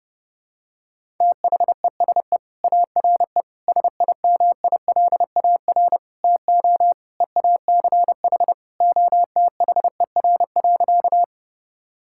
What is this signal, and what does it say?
Morse code, heard as a keyed tone, about 30 wpm, 700 Hz. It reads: THESE ARE SIMILAR TO EACH OTHER.
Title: Sentences from Top 1000 Words - Review Part 5 - 30wpm